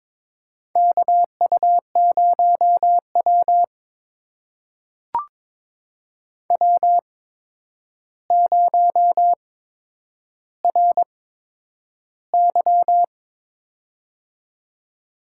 Morse code, heard as a keyed tone, about 22 wpm, spaced as for 9 wpm, 700 Hz. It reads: KU0W W0RY
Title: USA Call Signs - 4 Characters - 8x Character Spacing - 22wpm